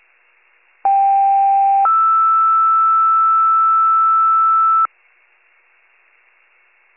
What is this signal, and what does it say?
Two-tone page, heard as a one-second tone, then a three-second tone, about 790 then 1320 Hz. It is heard from 0.8 to 4.9 s.